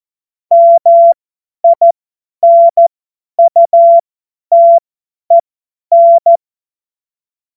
Morse code, heard as a keyed tone, about 14 wpm, spaced as for 10 wpm, 680 Hz. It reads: MINUTEN
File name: Morsekurs / Witze / Witz 25. Oktober 2022 50 F70